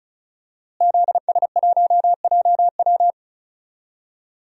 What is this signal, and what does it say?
Morse code, heard as a keyed tone, about 36 wpm, 700 Hz. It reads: ZS1JW
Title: DXCC-100 Call Signs - 5 Characters - Repeated 3x - 35wpm